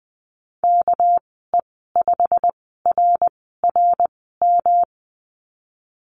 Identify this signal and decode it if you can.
Morse code, heard as a keyed tone, about 19 wpm, spaced as for 14 wpm, 700 Hz. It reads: KE5RRM
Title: 200 Lesson 32g - Letter k - Call Signs 20wpm